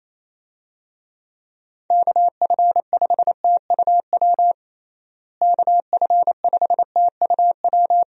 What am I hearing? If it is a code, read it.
Morse code, heard as a keyed tone, about 28 wpm, 700 Hz. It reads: KF5TUW KF5TUW
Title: USA Call Signs - 6 Characters - Repeated 3x - 28wpm